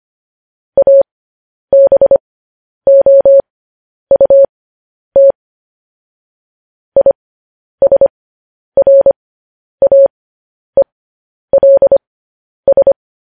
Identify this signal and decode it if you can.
Morse code, heard as a keyed tone, about 25 wpm, spaced as for 10 wpm, 550 Hz. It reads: ABOUT ISRAELS
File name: podcast / News headlines-1x-25x10